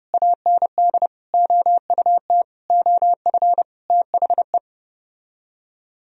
Morse code, heard as a keyed tone, about 30 wpm, 700 Hz. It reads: AND OUT OF THE